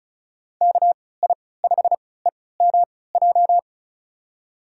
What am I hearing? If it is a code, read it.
Morse code, heard as a keyed tone, about 36 wpm, spaced as for 20 wpm, 700 Hz. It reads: KI5EMJ
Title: USA Call Signs - 3x Character Spacing - 35wpm